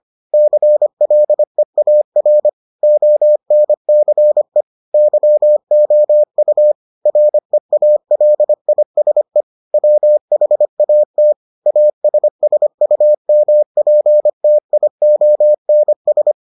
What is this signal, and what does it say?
Morse code, heard as a keyed tone, about 25 wpm, 600 Hz. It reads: CLEAR ONCE YOU REALISE WHAT ASSUMPTIONS